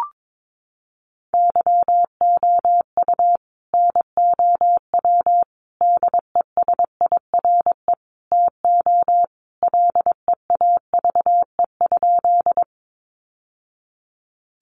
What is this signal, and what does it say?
Morse code, heard as a keyed tone, about 22 wpm, 700 Hz. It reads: YOU NOW DESIRE TO LEAVE?
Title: Sentences from Top 1100 Words 22wpm